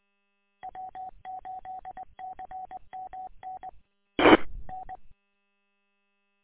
Morse code, heard as a keyed tone, about 20 wpm, 750 Hz. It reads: W8CMN T N
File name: w8cmn- -combined